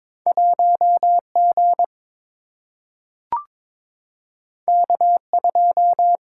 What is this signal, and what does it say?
Morse code, heard as a keyed tone, about 22 wpm, 700 Hz. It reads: 1G K2
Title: Single-Letter Single-Number 22wpm